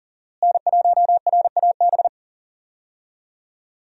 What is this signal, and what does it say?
Morse code, heard as a keyed tone, about 40 wpm, 700 Hz. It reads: N1RAB
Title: USA Call Signs - 5 Characters - Repeated 2x - 40wpm